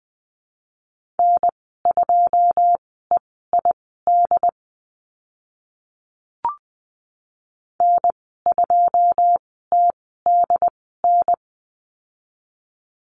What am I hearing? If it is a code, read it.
Morse code, heard as a keyed tone, about 20 wpm, spaced as for 15 wpm, 700 Hz. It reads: N2EID N2TDN
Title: 088 Lesson 15g - Number 5 - Call Signs 20wpm